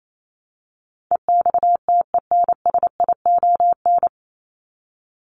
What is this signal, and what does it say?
Morse code, heard as a keyed tone, about 28 wpm, 700 Hz. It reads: EXTENSION
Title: Top 1501-1750 Words 28wpm